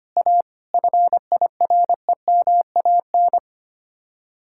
Morse code, heard as a keyed tone, about 32 wpm, 700 Hz. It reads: A FIREMAN